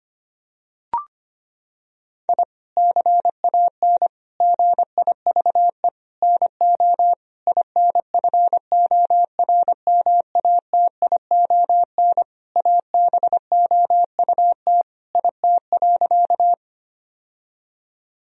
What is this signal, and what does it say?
Morse code, heard as a keyed tone, about 25 wpm, 700 Hz. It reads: I CAN GIVE NO INFORMATION ABOUT IT.
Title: Sentences from Top 1000 Words - Review Part 3 - 25wpm